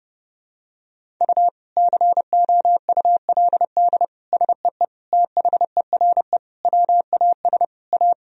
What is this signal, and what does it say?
Morse code, heard as a keyed tone, about 30 wpm, 700 Hz. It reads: U COULD SEE THERE WAS A
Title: Sentences from Top 500 Words and Abbreviations 30wpm